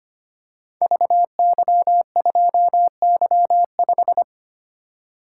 Morse code, heard as a keyed tone, about 25 wpm, 700 Hz. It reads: VY2Y5
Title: Code Groups - 5 Letter:Numbers - 25wpm